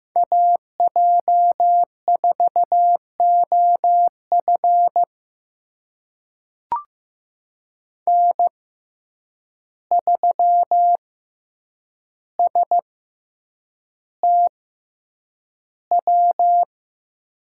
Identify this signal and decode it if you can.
Morse code, heard as a keyed tone, about 15 wpm, spaced as for 8 wpm, 700 Hz. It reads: AJ4OF N3STW
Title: USA Call Signs - 5 Characters - 6x Character Spacing - 15wpm